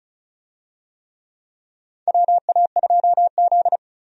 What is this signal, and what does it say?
Morse code, heard as a keyed tone, about 34 wpm, 700 Hz. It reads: WA2Z
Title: USA Call Signs - 4 Characters - 35wpm